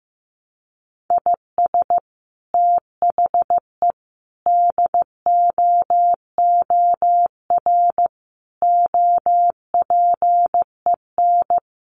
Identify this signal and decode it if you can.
Morse code, heard as a keyed tone, about 15 wpm, 700 Hz. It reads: IS THE DOOR OPEN
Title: Sentences from Top 1300 Words 15wpm